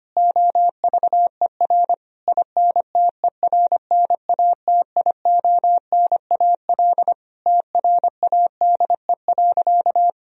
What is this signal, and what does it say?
Morse code, heard as a keyed tone, about 25 wpm, 700 Hz. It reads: OVER INTERNATIONAL TRADE.
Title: Sentences from Top 1000 Words - Review Part 4 - 25wpm